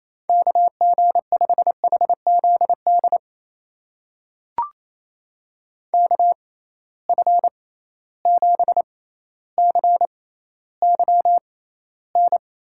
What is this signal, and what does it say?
Morse code, heard as a keyed tone, about 28 wpm, spaced as for 14 wpm, 700 Hz. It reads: KG5HZD KF7CYN
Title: USA Call Signs - 6 Characters - 6x Character Spacing - 28wpm